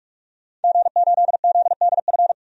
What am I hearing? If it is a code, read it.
Morse code, heard as a keyed tone, about 46 wpm, 700 Hz. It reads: G8ZDF
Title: DXCC-100 Call Signs - 5 Characters - 45wpm